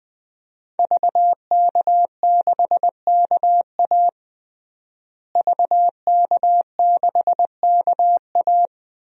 Morse code, heard as a keyed tone, about 20 wpm, 700 Hz. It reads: VK6KA VK6KA
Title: DXCC-100 Call Signs - 5 Characters - Repeated 2x - 20wpm